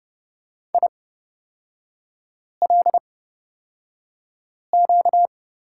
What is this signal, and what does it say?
Morse code, heard as a keyed tone, about 30 wpm, spaced as for 5 wpm, 700 Hz. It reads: ILQ
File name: Lesson 38e - Letter q - Three Characters 30wpm